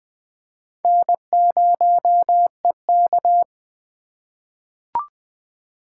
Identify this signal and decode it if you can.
Morse code, heard as a keyed tone, about 21 wpm, 700 Hz. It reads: N0EK
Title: USA Call Signs - 2x Character Spacing - 20wpm